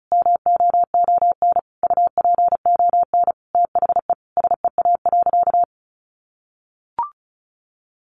Morse code, heard as a keyed tone, about 34 wpm, 700 Hz. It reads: MOON UPON THE SEA.
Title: Sentences from Top 1500 Words 35wpm